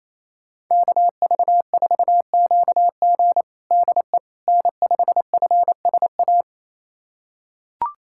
Morse code, heard as a keyed tone, about 27 wpm, 690 Hz. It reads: KV4QG DE N5FSA
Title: USA Callsign DE Callsign - 5 Characters - 28wpm